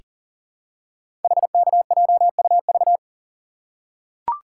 Morse code, heard as a keyed tone, about 40 wpm, 700 Hz. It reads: HKJUV
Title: Code Groups - 5 Letters - 40wpm